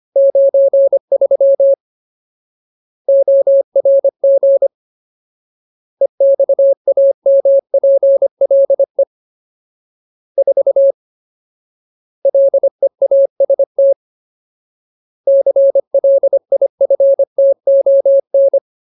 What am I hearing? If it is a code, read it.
Morse code, heard as a keyed tone, about 25 wpm, 550 Hz. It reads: 93 ORG EXAMPLE 4 LEAST CLIFTON
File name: Ultimate Word List / 25WPM Ultimate